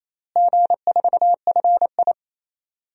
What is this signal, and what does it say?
Morse code, heard as a keyed tone, about 28 wpm, 700 Hz. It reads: G4FI
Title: DXCC-100 Call Signs - 4 Characters - Repeated 3x - 28wpm